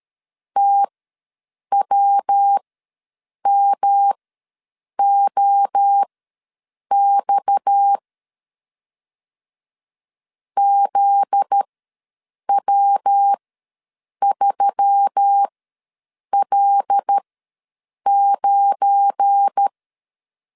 Morse code, heard as a keyed tone, about 13 wpm, spaced as for 7 wpm, 800 Hz. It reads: TWMOX ZW3L9
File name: LEZIONI / 8 YJ45 / 8 Y ws 12wpm 3409